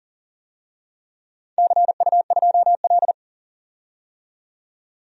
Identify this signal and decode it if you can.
Morse code, heard as a keyed tone, about 41 wpm, 700 Hz. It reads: CU2L
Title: DXCC-100 Call Signs - 4 Characters - 40wpm